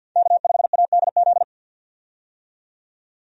Morse code, heard as a keyed tone, about 52 wpm, 700 Hz. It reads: K5ADZ
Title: USA Call Signs - 5 Characters - 50wpm